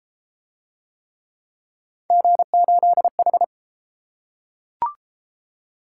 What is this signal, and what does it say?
Morse code, heard as a keyed tone, about 30 wpm, 700 Hz. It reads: G8H